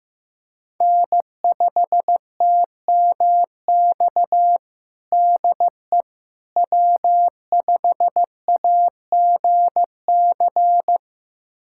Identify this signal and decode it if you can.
Morse code, heard as a keyed tone, about 15 wpm, 700 Hz. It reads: N5TMX DE W5AGC
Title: USA Callsign DE Callsign - 5 Characters - 15wpm